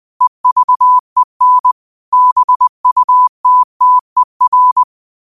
Morse code, heard as a keyed tone, about 20 wpm, 1000 Hz. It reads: EVEN BUTTER
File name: qotdfiles / 20wpm